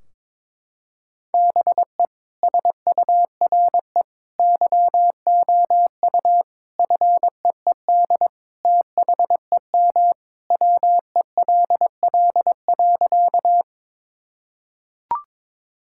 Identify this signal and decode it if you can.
Morse code, heard as a keyed tone, about 22 wpm, 700 Hz. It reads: BE SURE YOU FEED THEM WELL.
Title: Sentences from Top 1600 Words 22wpm